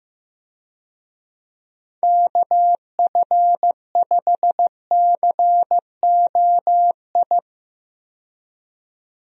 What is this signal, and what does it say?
Morse code, heard as a keyed tone, about 15 wpm, 700 Hz. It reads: KF5COI
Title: USA Call Signs - 6 Characters - 15wpm